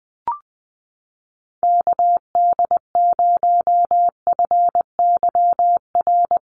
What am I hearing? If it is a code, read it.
Morse code, heard as a keyed tone, about 20 wpm, 700 Hz. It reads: KD0FYR
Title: USA Call Signs Repeated 2x 20wpm